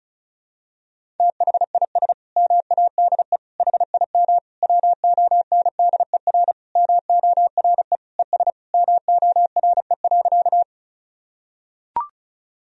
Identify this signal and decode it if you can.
Morse code, heard as a keyed tone, about 36 wpm, 700 Hz. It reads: THIS MADE HIM WONDER MORE ES MORE.
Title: Sentences from Top 500 Words and Abbreviations 35wpm